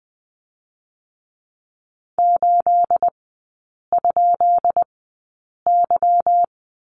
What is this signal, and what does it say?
Morse code, heard as a keyed tone, about 20 wpm, spaced as for 8 wpm, 700 Hz. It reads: 8?Y
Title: 218 Lesson 35e - Number 0 - Three Characters 20wpm